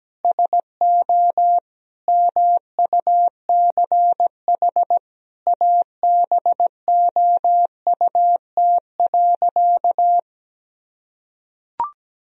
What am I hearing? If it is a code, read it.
Morse code, heard as a keyed tone, about 17 wpm, 700 Hz. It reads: SO MUCH ABOUT.
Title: Sentences from Top 1000 Words - Review Part 2 - 17wpm